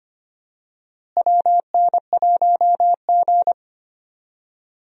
Morse code, heard as a keyed tone, about 25 wpm, 700 Hz. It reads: WN1G
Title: USA Call Signs - 4 Characters - Repeated 3x - 25wpm